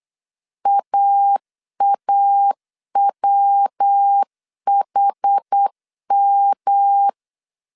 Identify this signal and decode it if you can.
Morse code, heard as a keyed tone, about 8 wpm, 800 Hz. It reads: AAWHM
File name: CW / DET / 1 3 1500C S B 8wpm 4141